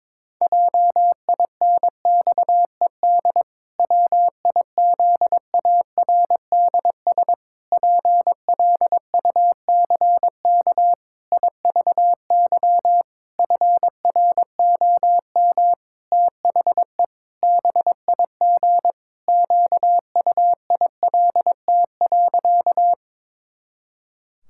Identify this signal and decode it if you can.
Morse code, heard as a keyed tone, about 22 wpm, 700 Hz. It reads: JINXED WIZARDS PLUCK IVY FROM THE BIG QUILT.
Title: Pangram Challenge v2 - 22wpm